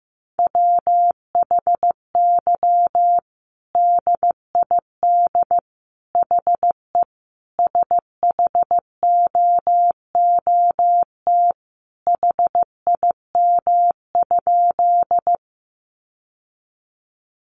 Morse code, heard as a keyed tone, about 14 wpm, 700 Hz. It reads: WHY DID HE SHOOT HIM?
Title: Sentences from Top 1200 Words 15wpm